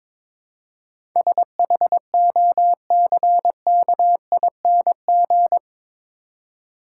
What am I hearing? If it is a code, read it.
Morse code, heard as a keyed tone, about 22 wpm, 700 Hz. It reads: SHOCKING
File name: practice / 22wpm / Eight Letter Words - 22wpm